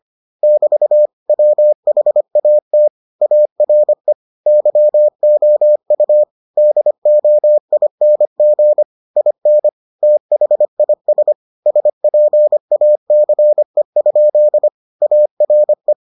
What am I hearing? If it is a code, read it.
Morse code, heard as a keyed tone, about 25 wpm, 600 Hz. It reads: <BT> WHAT ARE YOU DOING IN THIS SPACE? ARE